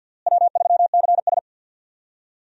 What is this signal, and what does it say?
Morse code, heard as a keyed tone, about 53 wpm, 700 Hz. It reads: W3CS
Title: USA Call Signs - 4 Characters - Repeated 2x - 50wpm